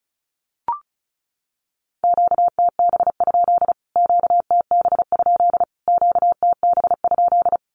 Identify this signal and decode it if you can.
Morse code, heard as a keyed tone, about 34 wpm, 700 Hz. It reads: QTB? QTB? QTB?
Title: Q-Codes Repeated 3x 35wpm